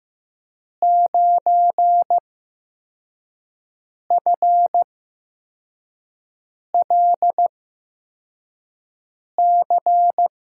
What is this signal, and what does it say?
Morse code, heard as a keyed tone, about 15 wpm, spaced as for 4 wpm, 700 Hz. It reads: 9FLC
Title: USA Call Signs - 5 Characters - 8x Character Spacing - 15wpm